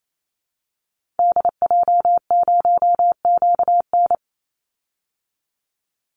Morse code, heard as a keyed tone, about 28 wpm, 700 Hz. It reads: DJ0QN
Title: Call Signs Repeated 2x - 28wpm